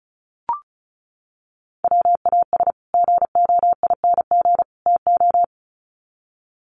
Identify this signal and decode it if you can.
Morse code, heard as a keyed tone, about 34 wpm, 700 Hz. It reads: WAS GOING TO
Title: Sets of 3 Words 35wpm